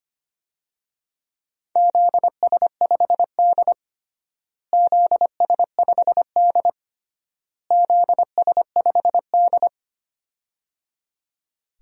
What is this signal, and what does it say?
Morse code, heard as a keyed tone, about 25 wpm, 700 Hz. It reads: ZS5D ZS5D ZS5D
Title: DXCC-100 Call Signs - 4 Characters - Repeated 3x - 25wpm